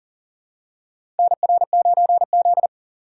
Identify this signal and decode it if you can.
Morse code, heard as a keyed tone, about 40 wpm, 700 Hz. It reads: NR9Z
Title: USA Call Signs - 4 Characters - 4x Character Spacing - 40wpm